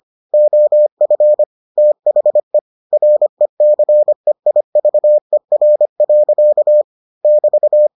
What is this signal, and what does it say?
Morse code, heard as a keyed tone, about 25 wpm, 600 Hz. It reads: OF THE RECEIVER. <BT>